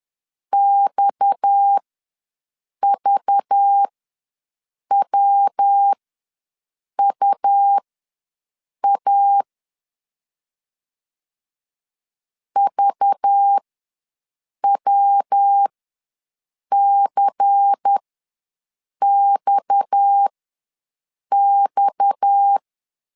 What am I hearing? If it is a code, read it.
Morse code, heard as a keyed tone, about 10 wpm, spaced as for 6 wpm, 800 Hz. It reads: XVWUA VWCXX
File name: LEZIONI / 6 PKX0 / 6 PKX ws 10wpm 3706